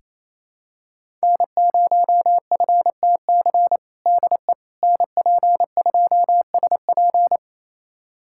Morse code, heard as a keyed tone, about 28 wpm, 700 Hz. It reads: N0FTC DE NP2SP
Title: USA Callsign DE Callsign - 5 Characters - 28wpm